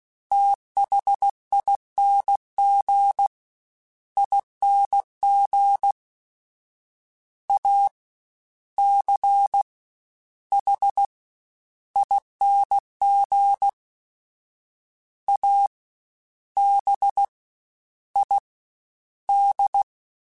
Morse code, heard as a keyed tone, about 16 wpm, spaced as for 11 wpm, 800 Hz. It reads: THING ING ACHING ABID
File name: Special INGs-mp3s / Special INGs-mp3s / Fast ing @15